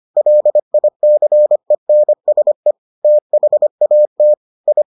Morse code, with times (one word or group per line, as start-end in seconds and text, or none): 0.2-2.7 LICENSE
3.0-4.3 THAT
4.7-4.8 I